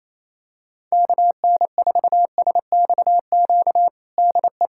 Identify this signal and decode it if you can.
Morse code, heard as a keyed tone, about 28 wpm, 700 Hz. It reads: KN4SXQ DE
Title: USA Callsign DE Callsign 28wpm